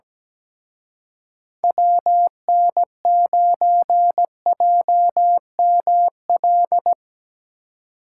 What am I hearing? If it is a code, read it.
Morse code, heard as a keyed tone, about 19 wpm, 700 Hz. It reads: WN9JML
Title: Speed Racing - USA 2x3 Call Signs 17wpm